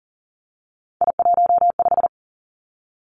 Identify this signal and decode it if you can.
Morse code, heard as a keyed tone, about 38 wpm, 700 Hz. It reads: I15
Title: Three Challenging Letter-Number Combinations 40wpm